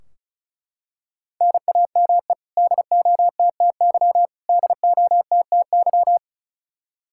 Morse code, heard as a keyed tone, about 36 wpm, 700 Hz. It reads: NAME DOTTY DOTTY
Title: Practice QSO - Exchange 1 - Beginning - 35wpm